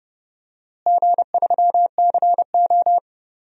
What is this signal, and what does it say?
Morse code, heard as a keyed tone, about 30 wpm, 700 Hz. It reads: G3CO
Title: DXCC-100 Call Signs - 4 Characters - Repeated 2x - 30wpm